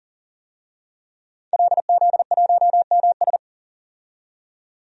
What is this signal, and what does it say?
Morse code, heard as a keyed tone, about 40 wpm, 700 Hz. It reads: LZ1MS
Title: DXCC-100 Call Signs - 5 Characters - 40wpm